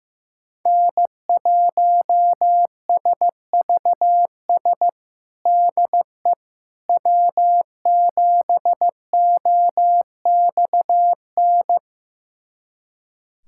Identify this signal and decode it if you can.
Morse code, heard as a keyed tone, about 15 wpm, 700 Hz. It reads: N1SVS DE W7OXN
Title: USA Callsign DE Callsign - 5 Characters - 15wpm